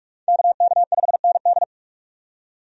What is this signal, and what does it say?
Morse code, heard as a keyed tone, about 46 wpm, 700 Hz. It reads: KK5ND
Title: USA Call Signs - 5 Characters - 45wpm